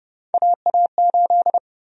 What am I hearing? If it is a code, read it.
Morse code, heard as a keyed tone, about 30 wpm, 700 Hz. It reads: AA8
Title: USA Call Sign Prefixes - 30wpm